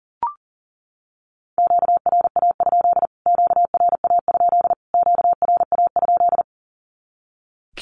Morse code, heard as a keyed tone, about 40 wpm, 700 Hz. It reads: QRA? QRA? QRA?
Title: Q-Codes Repeated 3x 40wpm